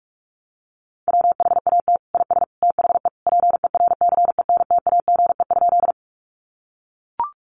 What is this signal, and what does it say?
Morse code, heard as a keyed tone, about 44 wpm, 700 Hz. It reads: WHAT IS THE PERCENTAGE?
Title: Sentences from Top 1300 Words 45wpm